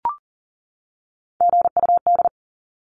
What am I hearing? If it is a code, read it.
Morse code, heard as a keyed tone, about 40 wpm, 700 Hz. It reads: GUD
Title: Abbreviations 40wpm